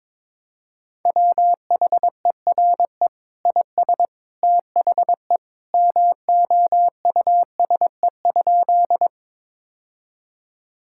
Morse code, heard as a keyed tone, about 22 wpm, 700 Hz. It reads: WHERE IS THE MOUSE?
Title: Sentences from Top 1000 Words - Review Part 5 - 22wpm